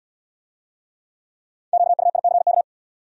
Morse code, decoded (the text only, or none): QUICK